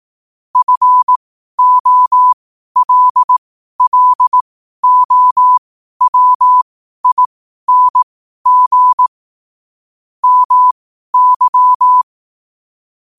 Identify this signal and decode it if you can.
Morse code, heard as a keyed tone, about 18 wpm, spaced as for 12 wpm, 1000 Hz. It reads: FOLLOWING MY
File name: qotdfiles / 12wpm